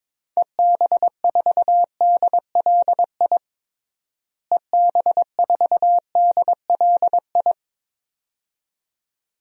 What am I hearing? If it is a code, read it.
Morse code, heard as a keyed tone, about 22 wpm, 700 Hz. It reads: EB4DLI EB4DLI
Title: DXCC-100 Call Signs - 6 Characters - Repeated 2x - 22wpm